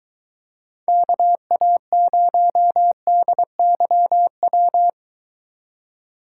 Morse code, heard as a keyed tone, about 24 wpm, 700 Hz. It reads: KA0DYW